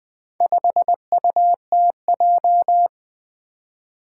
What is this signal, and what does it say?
Morse code, heard as a keyed tone, about 20 wpm, 700 Hz. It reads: 5UTJ
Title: Four Challenging Letter-Number Combinations 20wpm